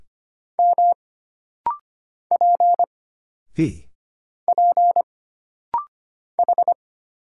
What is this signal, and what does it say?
Morse code, heard as a keyed tone, about 25 wpm, 700 Hz. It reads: M P P H
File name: Single Character (Letter:Number:Punctuation:Prosign) - Rapid-Fire - 25wpm